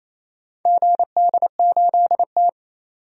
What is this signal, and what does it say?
Morse code, heard as a keyed tone, about 28 wpm, 700 Hz. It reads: GD8T
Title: DXCC-100 Call Signs - 4 Characters - 28wpm